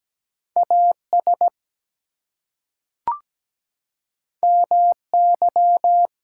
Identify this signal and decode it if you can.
Morse code, heard as a keyed tone, about 17 wpm, 700 Hz. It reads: AS MY
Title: Two Letter Words - 17wpm